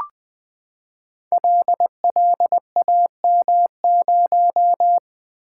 Morse code, heard as a keyed tone, about 20 wpm, 700 Hz. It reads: LLAM0
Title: Code Groups - 5 Letter:Numbers - 20wpm